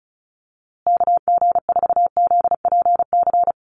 Morse code, heard as a keyed tone, about 34 wpm, 700 Hz. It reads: KG4ZPC
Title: USA Call Signs Repeated 2x 35wpm